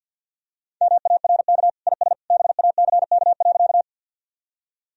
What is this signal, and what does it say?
Morse code, heard as a keyed tone, about 48 wpm, 700 Hz. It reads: MARK IS BACK.
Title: Sentences from Top 1000 Words - Review Part 1 - 50wpm